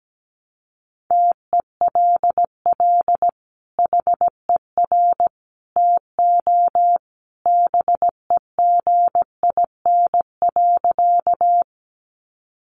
Morse code, heard as a keyed tone, about 17 wpm, 700 Hz. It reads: TELL HER TO BEGIN.
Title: Sentences from Top 300 Words 17wpm